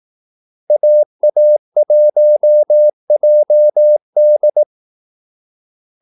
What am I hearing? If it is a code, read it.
Morse code, heard as a keyed tone, about 18 wpm, 600 Hz. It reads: AA1JD
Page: Kaldesignaler 18 wpm